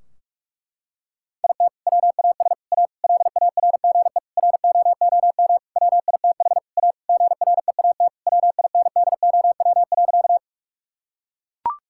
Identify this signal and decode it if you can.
Morse code, heard as a keyed tone, about 42 wpm, 700 Hz. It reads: IT WAS A LARGE ROOM WITH A GREAT WINDOW.
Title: Sentences from Top 1600 Words 45wpm